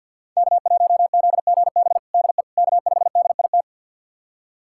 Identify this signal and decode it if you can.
Morse code, heard as a keyed tone, about 52 wpm, 700 Hz. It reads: K1ZGB DE K5DIT
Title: USA Callsign DE Callsign - 5 Characters - 50wpm